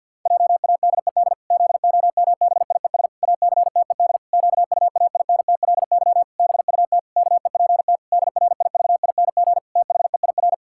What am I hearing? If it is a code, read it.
Morse code, heard as a keyed tone, about 50 wpm, 700 Hz. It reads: JADED ZOMBIES ACTED QUAINTLY BUT KEPT DRIVING THEIR
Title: Pangram Challenge v2 - 50wpm